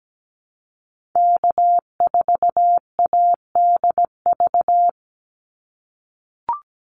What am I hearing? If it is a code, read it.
Morse code, heard as a keyed tone, about 17 wpm, 700 Hz. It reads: K4ADV
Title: NA SOTA Activator Callsigns 17wpm